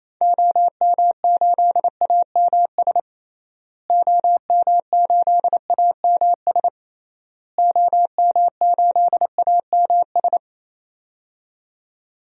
Morse code, heard as a keyed tone, about 28 wpm, 700 Hz. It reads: OM8AMS OM8AMS OM8AMS
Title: DXCC-100 Call Signs - 6 Characters - Repeated 3x - 28wpm